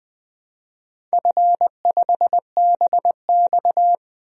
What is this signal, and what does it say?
Morse code, heard as a keyed tone, about 20 wpm, 700 Hz. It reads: F5BX
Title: DXCC-100 Call Signs - 4 Characters - 20wpm